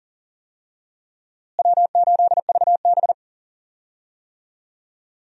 Morse code, heard as a keyed tone, about 38 wpm, 700 Hz. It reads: W8VB
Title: USA Call Signs - 4 Characters - Repeated 3x - 40wpm